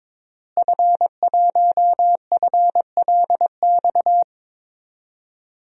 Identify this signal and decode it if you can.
Morse code, heard as a keyed tone, about 22 wpm, 700 Hz. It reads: F1FLX
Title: DXCC-100 Call Signs - 5 Characters - Repeated 3x - 22wpm